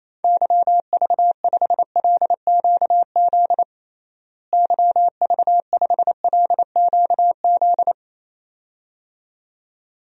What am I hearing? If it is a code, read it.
Morse code, heard as a keyed tone, about 28 wpm, 700 Hz. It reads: YV5LQZ YV5LQZ
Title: DXCC-100 Call Signs - 6 Characters - Repeated 2x - 28wpm